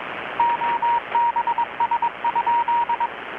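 Morse code, heard as a keyed tone, about 22 wpm, 940 Hz, with radio noise. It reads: OBS?
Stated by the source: Morse from KPH's Night of Nights XX maritime radiotelegraphy event, received in Chiapas, Mexico about 3,800 km away on a Kenwood R-600 with a grounded 10 m long-wire antenna.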